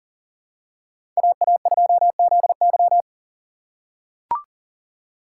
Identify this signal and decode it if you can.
Morse code, heard as a keyed tone, about 40 wpm, 700 Hz. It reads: AA2ZY
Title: USA Call Signs - 5 Characters - 3x Character Spacing - 40wpm